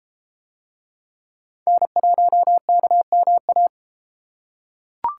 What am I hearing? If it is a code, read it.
Morse code, heard as a keyed tone, about 31 wpm, 700 Hz. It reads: N1KMA